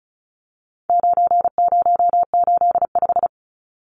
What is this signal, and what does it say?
Morse code, heard as a keyed tone, about 34 wpm, 700 Hz. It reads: 9085
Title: Numbers - 4-Digit 35wpm